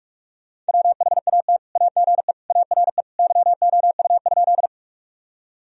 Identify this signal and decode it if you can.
Morse code, heard as a keyed tone, about 42 wpm, 700 Hz. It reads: WHAT AGE ARE YOU?
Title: Sentences from Top 1000 Words - Review Part 3 - 45wpm